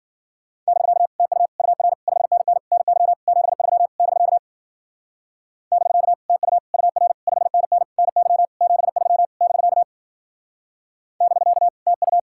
Morse code, heard as a keyed tone, about 52 wpm, 700 Hz. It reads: <BK> TU UR 5NN NY 73 <BK> <BK> TU UR 5NN NY 73 <BK> <BK> TU